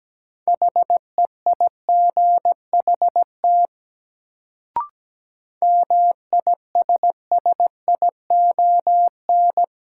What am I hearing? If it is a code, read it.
Morse code, heard as a keyed tone, about 17 wpm, 700 Hz. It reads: HEIGHT MISSION
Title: Sentences from Top 1600 Words - New Word Drill - 17wpm